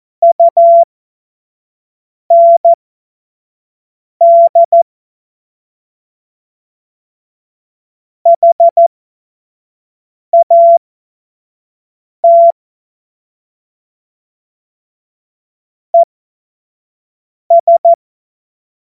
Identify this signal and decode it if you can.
Morse code, heard as a keyed tone, about 14 wpm, spaced as for 5 wpm, 680 Hz. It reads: UND HAT ES
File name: Witz-05-August-2025-25-F70